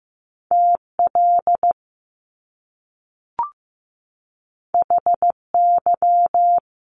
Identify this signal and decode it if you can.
Morse code, heard as a keyed tone, about 15 wpm, 700 Hz. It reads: TL HY
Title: Top Two Letter Combinations in Words 15wpm